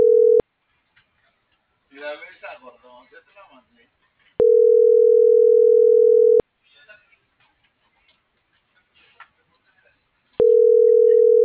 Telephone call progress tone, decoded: ringback tone

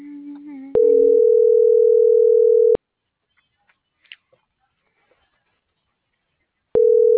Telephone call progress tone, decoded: ringback tone

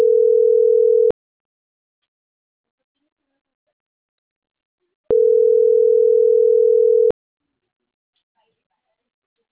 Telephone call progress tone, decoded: ringback tone

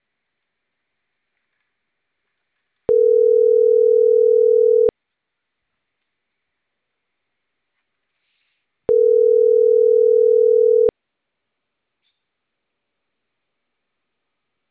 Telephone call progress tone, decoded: ringback tone

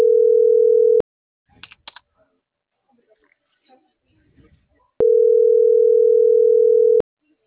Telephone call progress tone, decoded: ringback tone

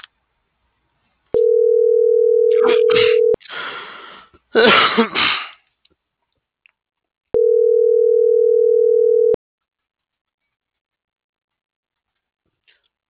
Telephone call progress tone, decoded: ringback tone

